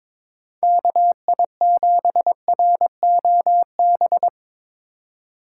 Morse code, heard as a keyed tone, about 24 wpm, 700 Hz. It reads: KI7ROB
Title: Speed Racing - USA 2x3 Call Signs 22wpm